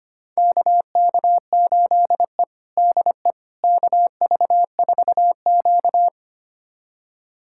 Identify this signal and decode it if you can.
Morse code, heard as a keyed tone, about 25 wpm, 700 Hz. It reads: KK8E DE KV4Q
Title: USA Callsign DE Callsign - 4 Characters - 25wpm